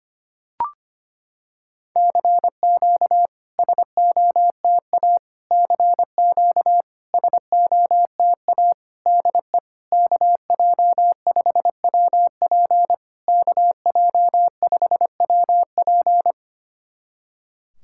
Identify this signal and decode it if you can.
Morse code, heard as a keyed tone, about 25 wpm, 690 Hz. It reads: CQ SOTA CQ SOTA DE KJ5WP KJ5WP
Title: SOTA QSO Practice 25wpm